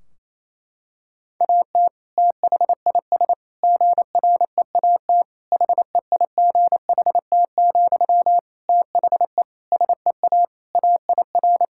Morse code, heard as a keyed tone, about 28 wpm, 700 Hz. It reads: AT THIS GREAT HEIGHT, THE SEA AIR
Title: Sentences from Top 1600 Words 28wpm